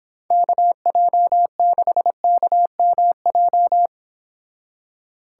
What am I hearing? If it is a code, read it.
Morse code, heard as a keyed tone, about 26 wpm, 700 Hz. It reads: KJ6KMJ